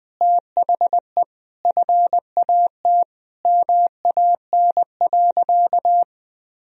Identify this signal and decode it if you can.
Morse code, heard as a keyed tone, about 21 wpm, 700 Hz. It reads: THE FAT MAN.
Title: Sentences from Top 1000 Words - Review Part 4 - 20wpm